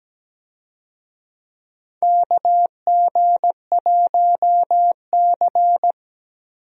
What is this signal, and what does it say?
Morse code, heard as a keyed tone, about 17 wpm, 700 Hz. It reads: KG1C
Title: USA Call Signs - 4 Characters - Repeated 3x - 17wpm